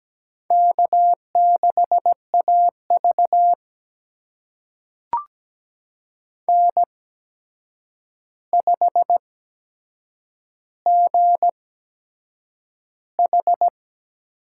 Morse code, heard as a keyed tone, about 17 wpm, spaced as for 7 wpm, 700 Hz. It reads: K6AV N5GH
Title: USA Call Signs - 4 Characters - 8x Character Spacing - 17wpm